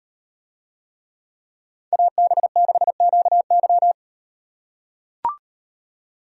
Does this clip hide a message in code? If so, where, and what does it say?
Yes: Morse code from 1.9 to 3.9 s, heard as a keyed tone, about 38 wpm, 700 Hz. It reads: AB6QY